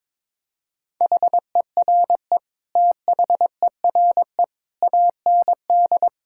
Morse code, heard as a keyed tone, about 22 wpm, 700 Hz. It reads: HERE THERE AND
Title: Sentences from Top 1600 Words 22wpm